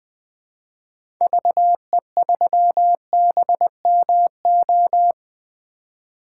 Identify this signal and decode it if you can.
Morse code, heard as a keyed tone, about 20 wpm, 700 Hz. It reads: VE3BMO
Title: 2023 POTA Activator Call Signs - 20wpm